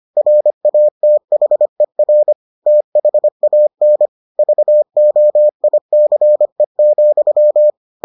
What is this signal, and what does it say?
Morse code, heard as a keyed tone, about 25 wpm, 600 Hz. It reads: RATHER THAN VOICE,